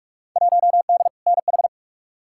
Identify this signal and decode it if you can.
Morse code, heard as a keyed tone, about 46 wpm, 700 Hz. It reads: 1D NH
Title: Field Day Exchanges Encore - 45wpm